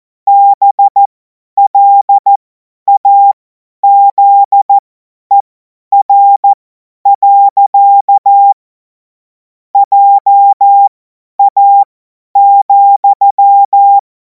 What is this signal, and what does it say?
Morse code, heard as a keyed tone, about 14 wpm, spaced as for 10 wpm, 800 Hz. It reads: BLAZER. JA,
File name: Morsekurs / Witze / Witz 17. Februar 2026 50 F70